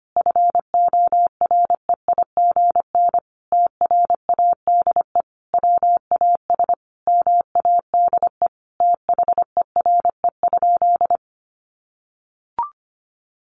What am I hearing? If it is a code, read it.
Morse code, heard as a keyed tone, about 25 wpm, 700 Hz. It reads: FOREIGN TRADE WAS MADE THERE?
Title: Sentences from Top 1400 Words 25wpm